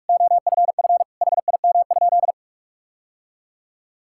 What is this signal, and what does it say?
Morse code, heard as a keyed tone, about 41 wpm, 700 Hz. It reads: OFF HIM?
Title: Sentences from Top 1600 Words 45wpm